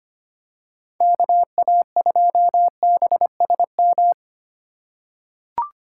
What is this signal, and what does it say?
Morse code, heard as a keyed tone, about 25 wpm, 700 Hz. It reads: KA2BSM
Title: USA Call Signs - 6 Characters - 3x Character Spacing - 25wpm